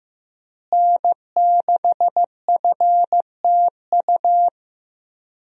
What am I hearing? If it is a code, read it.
Morse code, heard as a keyed tone, about 15 wpm, 700 Hz. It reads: N6FTU
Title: USA Call Signs - 5 Characters - Repeated 3x - 15wpm